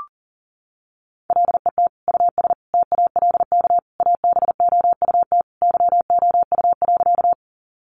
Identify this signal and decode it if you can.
Morse code, heard as a keyed tone, about 40 wpm, 700 Hz. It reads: LET US TALK ABOUT YOU.
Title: Sentences from Top 400 Words 40wpm